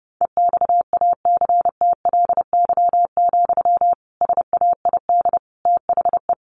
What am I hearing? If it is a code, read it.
Morse code, heard as a keyed tone, about 28 wpm, 700 Hz. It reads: EXACTLY, SAID THE
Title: Sentences from Top 1500 Words 30wpm